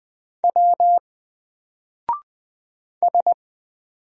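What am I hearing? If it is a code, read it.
Morse code, heard as a keyed tone, about 20 wpm, 700 Hz. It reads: W S
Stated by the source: Single letter and number Morse drill at 20 wpm, from N4LCD.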